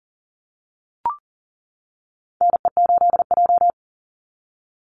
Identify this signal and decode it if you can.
Morse code, heard as a keyed tone, about 40 wpm, 700 Hz. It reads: NE8J
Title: Call Signs - 40wpm